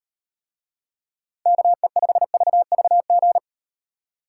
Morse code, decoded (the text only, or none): KE5VVG